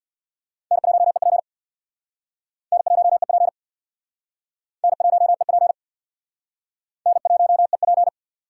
Morse code, heard as a keyed tone, about 62 wpm, 700 Hz. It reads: N1EP N1EP N1EP N1EP
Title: Speed Racing - USA 1x2 Call Signs 50wpm